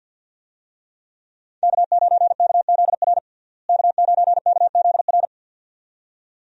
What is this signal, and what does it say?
Morse code, heard as a keyed tone, about 53 wpm, 700 Hz. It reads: K9KZR K9KZR
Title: USA Call Signs - 5 Characters - Repeated 2x - 50wpm